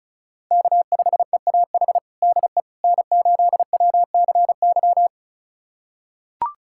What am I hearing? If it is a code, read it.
Morse code, heard as a keyed tone, about 34 wpm, 700 Hz. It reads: K5EAH DE N8WCY
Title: USA Callsign DE Callsign - 5 Characters - 35wpm